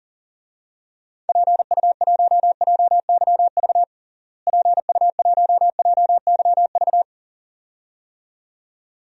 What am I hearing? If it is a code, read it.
Morse code, heard as a keyed tone, about 40 wpm, 700 Hz. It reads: PU1JYV PU1JYV
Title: DXCC-100 Call Signs - 6 Characters - Repeated 2x - 40wpm